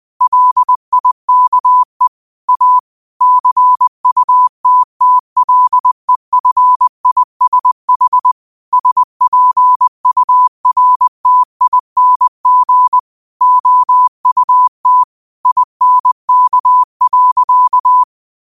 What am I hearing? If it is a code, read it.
Morse code, heard as a keyed tone, about 20 wpm, 1000 Hz. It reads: LIKE A CUTTLEFISH SPURTING OUT INK.
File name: qotdfiles / 20wpm